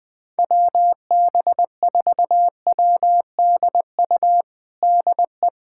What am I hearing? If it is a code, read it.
Morse code, heard as a keyed tone, about 21 wpm, 700 Hz. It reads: WB4WDU DE
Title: USA Callsign DE Callsign - 6 Characters - 20wpm